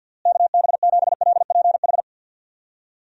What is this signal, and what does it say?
Morse code, heard as a keyed tone, about 52 wpm, 700 Hz. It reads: KB7LPH
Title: USA Call Signs - 6 Characters - Repeated 3x - 50wpm